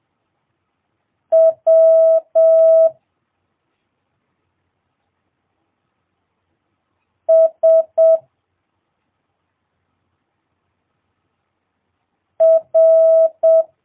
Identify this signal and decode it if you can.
Morse code, heard as a keyed tone, about 7 wpm, 650 Hz. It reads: W S R
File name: jh8cbh / 2022cw